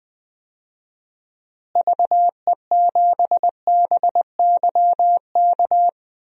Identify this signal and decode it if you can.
Morse code, heard as a keyed tone, about 21 wpm, 700 Hz. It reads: VE7BYK
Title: DXCC-100 Call Signs - 6 Characters - Repeated 2x - 20wpm